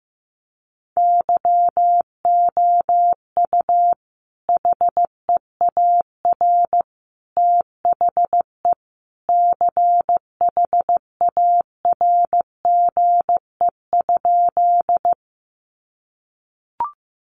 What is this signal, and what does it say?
Morse code, heard as a keyed tone, about 15 wpm, 700 Hz. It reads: YOU HEAR THE CHARGE?
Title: Sentences from Top 900 Words 15wpm